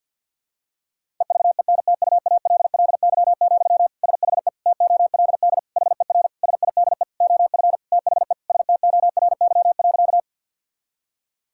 Morse code, heard as a keyed tone, about 52 wpm, 700 Hz. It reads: EVENTUALLY, SHE TOLD HER SIDE OF THE STORY.
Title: Sentences from Top 1000 Words - Review Part 6 - 50wpm